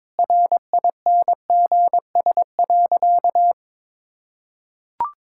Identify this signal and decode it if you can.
Morse code, heard as a keyed tone, about 22 wpm, 700 Hz. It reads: RINGS.
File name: practice / 22wpm / Easy Sentences with Hints - 22wpm